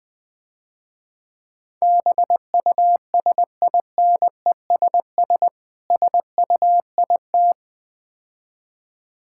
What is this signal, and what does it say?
Morse code, heard as a keyed tone, about 20 wpm, 690 Hz. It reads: BUSINESS SUIT
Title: Difficult Words - Speed-Racing with Hints - 20wpm